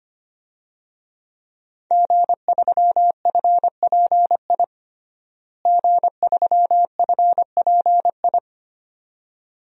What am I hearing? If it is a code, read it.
Morse code, heard as a keyed tone, about 25 wpm, 700 Hz. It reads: G3FPI G3FPI
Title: DXCC-100 Call Signs - 5 Characters - Repeated 2x - 25wpm